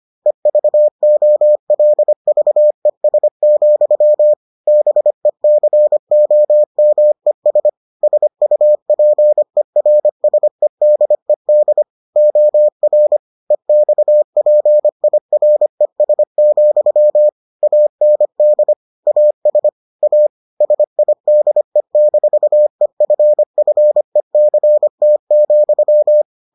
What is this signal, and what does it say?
Morse code, heard as a keyed tone, about 25 wpm, 600 Hz. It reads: EVOLVES, BECOMES SUPERSEDED OR EXPIRES, AND AS A SIDE-EFFECT,